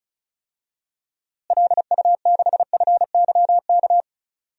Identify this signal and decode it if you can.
Morse code, heard as a keyed tone, about 36 wpm, 700 Hz. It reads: LU6FYK
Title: DXCC-100 Call Signs - 6 Characters - Repeated 3x - 35wpm